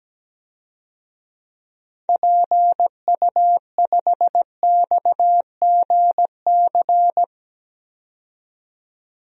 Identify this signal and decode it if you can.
Morse code, heard as a keyed tone, about 17 wpm, 700 Hz. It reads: PU5XGC
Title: DXCC-100 Call Signs - 6 Characters - 17wpm